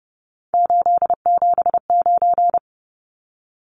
Morse code, heard as a keyed tone, about 30 wpm, 700 Hz. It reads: 879
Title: Numbers - 3-Digit 30wpm